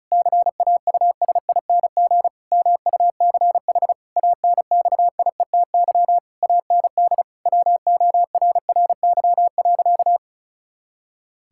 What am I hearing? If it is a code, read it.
Morse code, heard as a keyed tone, about 36 wpm, 700 Hz. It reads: CAUSING MUCH ANXIETY AND WORRY.
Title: Pangram Challenge v2 - 35wpm